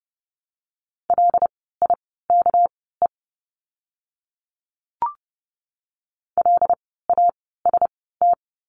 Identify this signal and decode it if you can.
Morse code, heard as a keyed tone, about 30 wpm, spaced as for 17 wpm, 700 Hz. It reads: LIKE LAST
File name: course / 30:10wpm / 237 Lesson 38f - Letter q - Words 30wpm